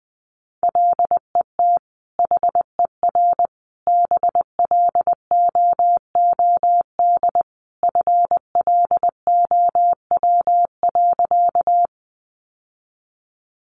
Morse code, heard as a keyed tone, about 20 wpm, 700 Hz. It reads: LET HER BLOOD FLOW.
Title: Sentences from Top 1200 Words 20wpm